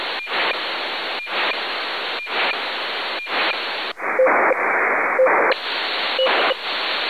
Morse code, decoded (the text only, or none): none